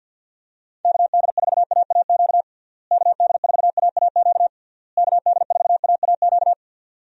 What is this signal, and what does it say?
Morse code, heard as a keyed tone, about 48 wpm, 700 Hz. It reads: KD4AAQ KD4AAQ KD4AAQ
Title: USA Call Signs - 6 Characters - Repeated 3x - 50wpm